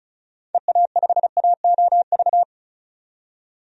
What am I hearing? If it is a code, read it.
Morse code, heard as a keyed tone, about 36 wpm, 700 Hz. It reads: EA5AOV
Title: DXCC-100 Call Signs - 6 Characters - 35wpm